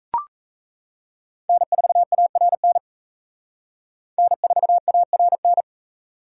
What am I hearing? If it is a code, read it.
Morse code, heard as a keyed tone, about 34 wpm, 700 Hz. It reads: N4ARN N4ARN
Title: Speed Racing - USA 1x3 Call Signs 28wpm